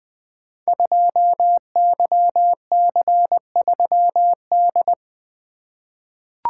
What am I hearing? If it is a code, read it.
Morse code, heard as a keyed tone, about 20 wpm, 700 Hz. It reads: 2YC3D
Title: Code Groups - 5 Letter:Numbers - 20wpm